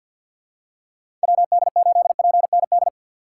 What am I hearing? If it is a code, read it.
Morse code, heard as a keyed tone, about 50 wpm, 700 Hz. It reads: WD8PND
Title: USA Call Signs - 6 Characters - 4x Character Spacing - 50wpm